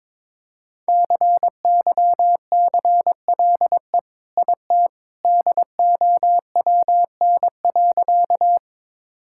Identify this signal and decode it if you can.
Morse code, heard as a keyed tone, about 22 wpm, 700 Hz. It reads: CYCLE IT DOWN.
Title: Sentences from Top 1000 Words - Review Part 6 - 22wpm